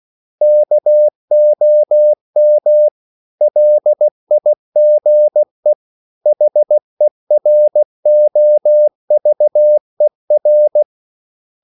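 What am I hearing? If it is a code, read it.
Morse code, heard as a keyed tone, about 16 wpm, 600 Hz. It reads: KOM LIGE HEROVER
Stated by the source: Slow (QRS) Morse code at 16 wpm, from CW med Gnister.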